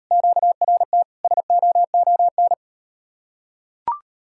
Morse code, heard as a keyed tone, about 38 wpm, 700 Hz. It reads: QRT SOON